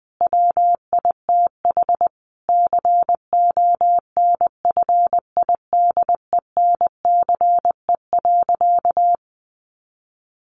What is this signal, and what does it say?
Morse code, heard as a keyed tone, about 19 wpm, 700 Hz. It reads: WITH CONFIDENCE.
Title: Sentences from Top 1500 Words 20wpm